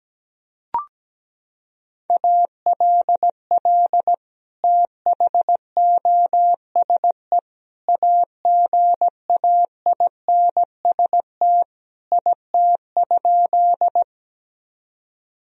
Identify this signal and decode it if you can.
Morse code, heard as a keyed tone, about 17 wpm, 700 Hz. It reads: ALL THOSE AGAINST IT?
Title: Sentences from Top 1000 Words - Review Part 1 - 17wpm